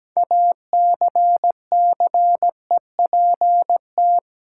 Morse code, heard as a keyed tone, about 17 wpm, 700 Hz. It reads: ACCEPT
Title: Sentences from Top 1300 Words - New Word Drill - 17wpm